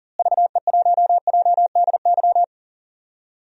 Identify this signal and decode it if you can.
Morse code, heard as a keyed tone, about 39 wpm, 700 Hz. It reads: VE1JDY
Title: DXCC-100 Call Signs - 6 Characters - Repeated 2x - 40wpm